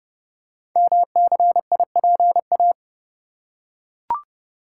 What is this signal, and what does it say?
Morse code, heard as a keyed tone, about 30 wpm, 700 Hz. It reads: MCIPA
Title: Code Groups - 5 Letters - 30wpm